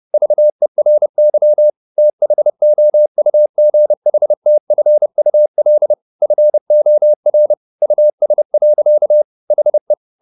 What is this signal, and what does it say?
Morse code, heard as a keyed tone, about 30 wpm, 600 Hz. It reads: VERY THOUGHTFUL FOR US. HE
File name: the old man and the sea / the old man and the sea 1 30wpm